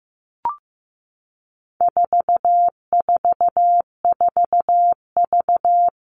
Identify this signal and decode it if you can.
Morse code, heard as a keyed tone, about 15 wpm, 700 Hz. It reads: 444V
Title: All Cut Numbers - 4-Digit 15wpm